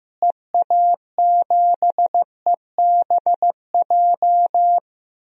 Morse code, heard as a keyed tone, about 15 wpm, 700 Hz. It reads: EA7EBJ
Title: DXCC-100 Call Signs - 6 Characters - Repeated 2x - 15wpm